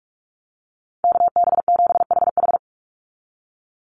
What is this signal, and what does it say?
Morse code, heard as a keyed tone, about 44 wpm, 690 Hz. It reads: KB7HH
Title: NA SOTA Chaser Callsigns 45wpm